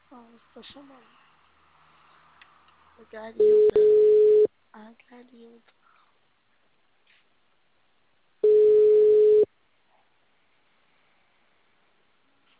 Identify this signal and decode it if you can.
Ringback tone (Europe, 425 Hz), with speech.
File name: out- -2025- - - .462